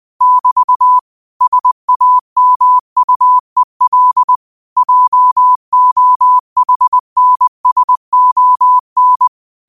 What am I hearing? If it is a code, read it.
Morse code, heard as a keyed tone, about 20 wpm, 1000 Hz. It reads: <BT> SAMUEL JOHNSON
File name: qotdfiles / 20wpm